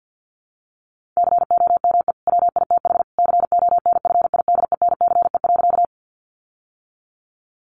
Morse code, heard as a keyed tone, about 50 wpm, 700 Hz. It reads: COME WITH CONFIDENCE.
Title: Sentences from Top 1500 Words 50wpm